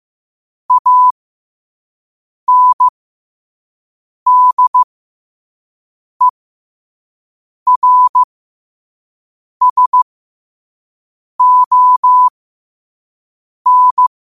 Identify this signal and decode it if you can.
Morse code, heard as a keyed tone, about 15 wpm, spaced as for 5 wpm, 1000 Hz. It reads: ANDERSON
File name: qotdfiles / 5wpm